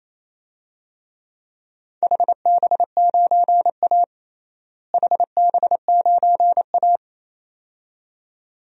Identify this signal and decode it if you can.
Morse code, heard as a keyed tone, about 28 wpm, 700 Hz. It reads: HB9A HB9A
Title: DXCC-100 Call Signs - 4 Characters - Repeated 2x - 28wpm